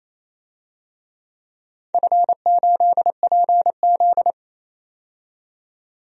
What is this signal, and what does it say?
Morse code, heard as a keyed tone, about 28 wpm, 700 Hz. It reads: F8PZ
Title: DXCC-100 Call Signs - 4 Characters - 28wpm